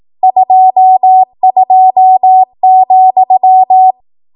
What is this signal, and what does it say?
Morse code, heard as a keyed tone, about 18 wpm, 750 Hz. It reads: 22,
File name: Archive / 18 WPM / 18WPM